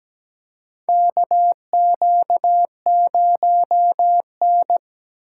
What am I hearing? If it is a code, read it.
Morse code, heard as a keyed tone, about 17 wpm, 700 Hz. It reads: KQ0N
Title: USA Call Signs - 8x Character Spacing - 17wpm